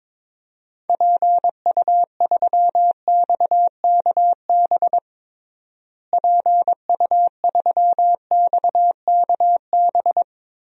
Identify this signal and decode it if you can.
Morse code, heard as a keyed tone, about 21 wpm, 700 Hz. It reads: PU3XKB PU3XKB
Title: DXCC-100 Call Signs - 6 Characters - Repeated 2x - 22wpm